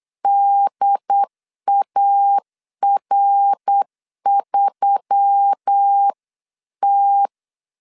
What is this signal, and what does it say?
Morse code, heard as a keyed tone, about 8 wpm, 800 Hz. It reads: DAR3T
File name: CW / DET / 1 6 1500C S B 8wpm 5142